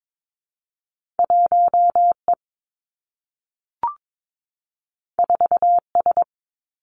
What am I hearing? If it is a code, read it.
Morse code, heard as a keyed tone, about 22 wpm, 700 Hz. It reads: 1 E 4 S
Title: Two Challenging Letter-Number Combinations 22wpm